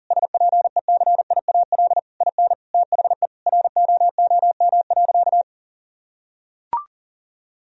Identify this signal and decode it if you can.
Morse code, heard as a keyed tone, about 40 wpm, 700 Hz. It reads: SPECIAL IN THE ROOM.
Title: Sentences from Top 1000 Words - Review Part 3 - 40wpm